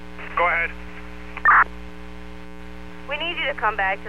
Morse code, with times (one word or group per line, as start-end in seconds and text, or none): none